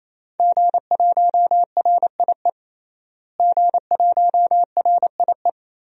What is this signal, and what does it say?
Morse code, heard as a keyed tone, about 27 wpm, 700 Hz. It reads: G1RIE G1RIE
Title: DXCC-100 Call Signs - 5 Characters - Repeated 3x - 28wpm